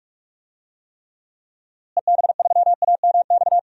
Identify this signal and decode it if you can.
Morse code, heard as a keyed tone, about 46 wpm, 700 Hz. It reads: EB3AMX
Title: DXCC-100 Call Signs - 6 Characters - Repeated 3x - 45wpm